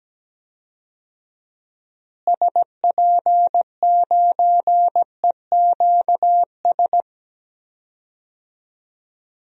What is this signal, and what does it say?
Morse code, heard as a keyed tone, about 17 wpm, 700 Hz. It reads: SP9EQS
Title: DXCC-100 Call Signs - 6 Characters - Repeated 3x - 17wpm